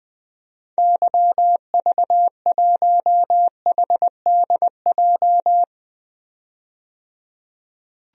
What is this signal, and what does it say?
Morse code, heard as a keyed tone, about 20 wpm, 700 Hz. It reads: YV1HDJ
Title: DXCC-100 Call Signs - 6 Characters - 20wpm